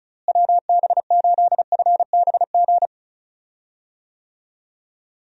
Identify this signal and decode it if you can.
Morse code, heard as a keyed tone, about 36 wpm, 700 Hz. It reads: WB8FBG
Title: USA Call Signs - 6 Characters - Repeated 2x - 35wpm